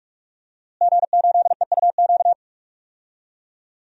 Morse code, heard as a keyed tone, about 43 wpm, 700 Hz. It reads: G8EUQ
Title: DXCC-100 Call Signs - 5 Characters - Repeated 3x - 45wpm